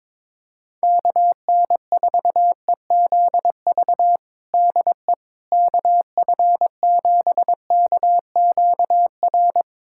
Morse code, heard as a keyed tone, about 22 wpm, 700 Hz. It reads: KN4EZV DE KF7KQR
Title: USA Callsign DE Callsign 22wpm